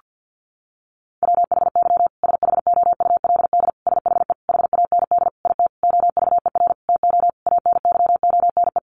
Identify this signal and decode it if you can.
Morse code, heard as a keyed tone, about 49 wpm, 700 Hz. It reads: WHY SHOULD SHE HAND IT OVER TO ANYONE
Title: Sentences from Top 900 Words 50wpm